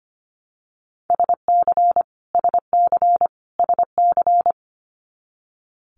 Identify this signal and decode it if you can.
Morse code, heard as a keyed tone, about 25 wpm, 700 Hz. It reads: SC SC SC
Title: Multi-letter Phonemes Repeated 3x 25wpm